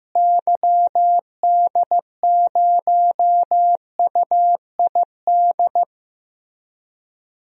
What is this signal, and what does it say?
Morse code, heard as a keyed tone, about 15 wpm, 700 Hz. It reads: YD0UID
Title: DXCC-100 Call Signs - 6 Characters - Repeated 3x - 15wpm